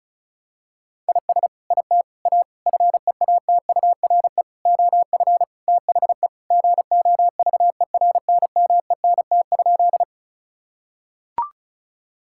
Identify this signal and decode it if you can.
Morse code, heard as a keyed tone, about 36 wpm, 700 Hz. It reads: IS IT A FEATURE OF THE GOVERNMENT?
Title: Sentences from Top 1000 Words - Review Part 5 - 35wpm